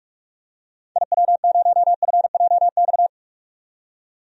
Morse code, heard as a keyed tone, about 42 wpm, 700 Hz. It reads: IW0FJX
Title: DXCC-100 Call Signs - 6 Characters - 45wpm